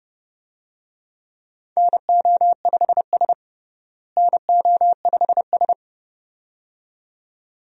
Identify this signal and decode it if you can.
Morse code, heard as a keyed tone, about 30 wpm, 700 Hz. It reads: NO5S NO5S
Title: USA Call Signs - 4 Characters - Repeated 2x - 30wpm